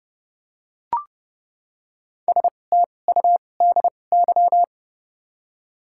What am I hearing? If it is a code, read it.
Morse code, heard as a keyed tone, about 32 wpm, spaced as for 22 wpm, 700 Hz. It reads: STUDY
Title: 2x Character Spacing - Top 300 Words - 30wpm